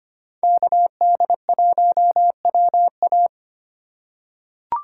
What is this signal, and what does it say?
Morse code, heard as a keyed tone, about 25 wpm, 700 Hz. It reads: KD1WA
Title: USA Call Signs - 3x Character Spacing - 25wpm